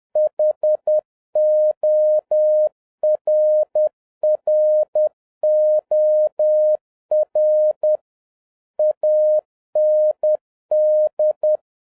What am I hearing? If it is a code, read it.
Morse code, heard as a keyed tone, about 10 wpm, 610 Hz. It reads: HORROR AND